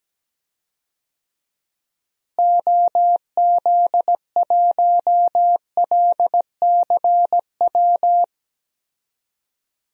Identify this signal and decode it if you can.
Morse code, heard as a keyed tone, about 17 wpm, 700 Hz. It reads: OZ1LCW
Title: DXCC-100 Call Signs - 6 Characters - Repeated 2x - 17wpm